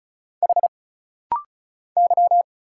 Morse code, heard as a keyed tone, about 33 wpm, 700 Hz. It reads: H Y
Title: Single Letters - Rapid-Fire 35wpm